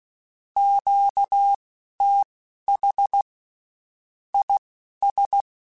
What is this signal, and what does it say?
Morse code, heard as a keyed tone, about 16 wpm, spaced as for 11 wpm, 800 Hz. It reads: QTH IS
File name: (spaced) / QSOs Spaced-mp3s / kc7haa @15 sp